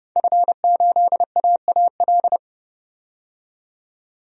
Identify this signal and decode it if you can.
Morse code, heard as a keyed tone, about 30 wpm, 700 Hz. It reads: F8AAL
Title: DXCC-100 Call Signs - 5 Characters - 30wpm